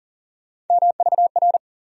Morse code, heard as a keyed tone, about 40 wpm, 700 Hz. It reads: MVR
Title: USA Call Sign Postfixes - 3 Letters 40wpm